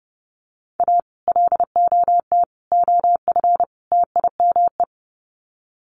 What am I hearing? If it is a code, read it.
Morse code, heard as a keyed tone, about 28 wpm, 700 Hz. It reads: A LOT OF TIME
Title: Sets of 4 Words 30wpm